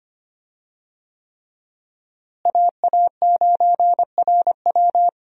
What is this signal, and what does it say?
Morse code, heard as a keyed tone, about 25 wpm, 700 Hz. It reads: AA9RW
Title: USA Call Signs - 5 Characters - Repeated 3x - 25wpm